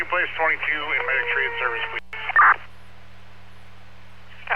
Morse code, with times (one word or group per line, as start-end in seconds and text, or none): none